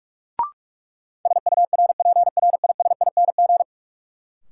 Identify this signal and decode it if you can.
Morse code, heard as a keyed tone, about 46 wpm, 700 Hz. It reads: SURPRISING
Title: Sets of 5 Words - New Word Drill - 45wpm